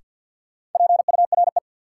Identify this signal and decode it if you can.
Morse code, heard as a keyed tone, about 47 wpm, 700 Hz. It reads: PURE
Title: Sentences from Top 1500 Words - New Word Drill Alt - 50wpm